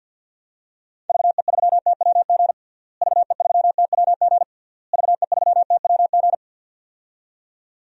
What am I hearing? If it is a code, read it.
Morse code, heard as a keyed tone, about 47 wpm, 700 Hz. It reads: VE3TWG VE3TWG VE3TWG
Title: DXCC-100 Call Signs - 6 Characters - Repeated 3x - 50wpm